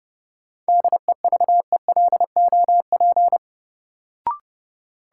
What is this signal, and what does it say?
Morse code, heard as a keyed tone, about 30 wpm, 700 Hz. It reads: DEVELOP
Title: Sentences from Top 1000 Words - Review Part 2 - New Word Drill - 30wpm